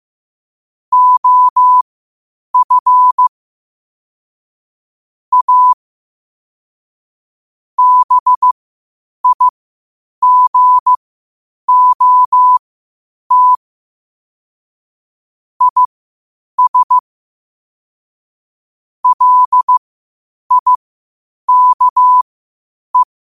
Morse code, heard as a keyed tone, about 15 wpm, spaced as for 8 wpm, 1000 Hz. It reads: OF A BIGOT IS LIKE
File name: qotdfiles / 8wpm